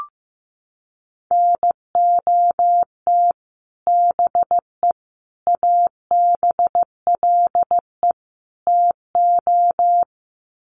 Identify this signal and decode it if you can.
Morse code, heard as a keyed tone, about 15 wpm, 700 Hz. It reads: NOT BE ABLE TO
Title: Sets of 4 Words 15wpm